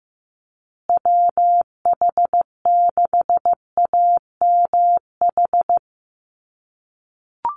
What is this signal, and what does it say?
Morse code, heard as a keyed tone, about 15 wpm, 700 Hz. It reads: WH6AMH
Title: USA Call Signs Repeated 3x 15wpm